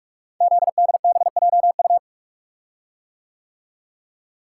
Morse code, heard as a keyed tone, about 44 wpm, 700 Hz. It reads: ZDBJU
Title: Code Groups - 5 Letters - 45wpm